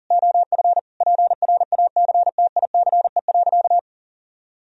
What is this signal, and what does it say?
Morse code, heard as a keyed tone, about 40 wpm, 700 Hz. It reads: OF PRACTICE.
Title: Sentences from Top 1000 Words - Review Part 4 - 40wpm